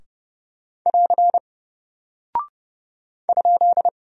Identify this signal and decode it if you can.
Morse code, heard as a keyed tone, about 30 wpm, 700 Hz. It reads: <AR> ?
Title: Single Character (Letter:Number:Punctuation:Prosign) Alt 30wpm